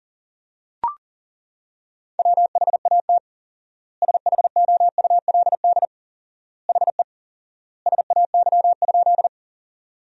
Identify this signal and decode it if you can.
Morse code, heard as a keyed tone, about 40 wpm, 700 Hz. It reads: WHAT SHOULD HE SAY?
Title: 4x Word-Spacing - Sentences from Top 300 Words - 40wpm